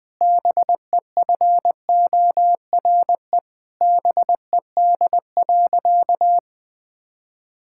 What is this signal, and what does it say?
Morse code, heard as a keyed tone, about 20 wpm, 700 Hz. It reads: BEFORE BED.
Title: Difficult Sentences with Hints - 20wpm